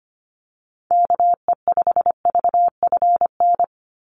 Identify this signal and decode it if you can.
Morse code, heard as a keyed tone, about 25 wpm, 700 Hz. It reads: KE5VFN
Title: USA Call Signs 25wpm